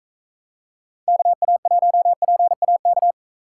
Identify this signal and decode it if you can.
Morse code, heard as a keyed tone, about 35 wpm, 700 Hz. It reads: KA1PAK
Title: Speed Racing - USA 2x3 Call Signs 28wpm